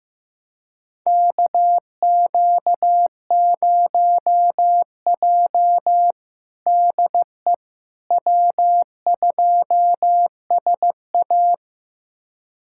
Morse code, heard as a keyed tone, about 15 wpm, 700 Hz. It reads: KQ0J DE W2SA
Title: USA Callsign DE Callsign - 4 Characters - 15wpm